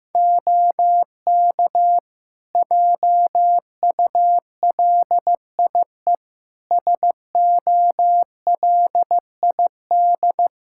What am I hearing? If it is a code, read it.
Morse code, heard as a keyed tone, about 15 wpm, 700 Hz. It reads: OK JULIE SOLID
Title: Practice QSO - Exchange 3 - More Info - 15wpm